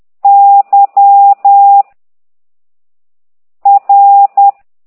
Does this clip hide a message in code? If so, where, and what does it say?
Yes: Morse code from 0.2 to 4.5 s, heard as a keyed tone, about 10 wpm, 800 Hz. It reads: Y R